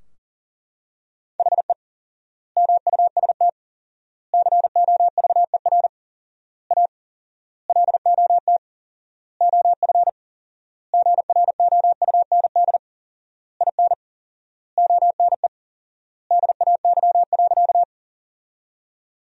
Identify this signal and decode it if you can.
Morse code, heard as a keyed tone, about 37 wpm, 700 Hz. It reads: HE MUST COVER A LOT OF GROUND IN ONE DAY.
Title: 4x Word-Spacing - Sentences from Top 500 Words - 40wpm